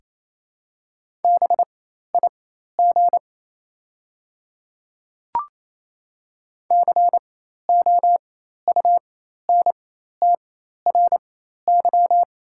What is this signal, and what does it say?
Morse code, heard as a keyed tone, about 27 wpm, spaced as for 13 wpm, 690 Hz. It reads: BIG COUNTRY
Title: 4x Character Spacing - Top 500 Words - 28wpm